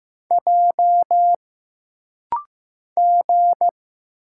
Morse code, heard as a keyed tone, about 15 wpm, 700 Hz. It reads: J G
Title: Single Letters - Rapid-Fire 15wpm